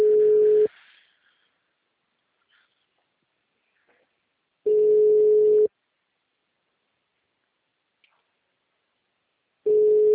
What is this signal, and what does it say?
Ringback tone (Europe, 425 Hz).
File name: out- -2011